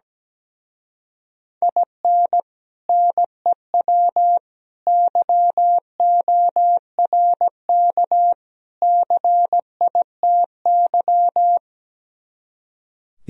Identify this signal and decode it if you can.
Morse code, heard as a keyed tone, about 17 wpm, 700 Hz. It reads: IN NEW YORK CITY
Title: Speed-Racing - Sets of 4 Words 17wpm